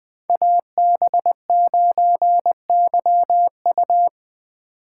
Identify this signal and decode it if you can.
Morse code, heard as a keyed tone, about 20 wpm, 700 Hz. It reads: AB9YU
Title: Speed Racing - USA 2x2 Call Signs 17wpm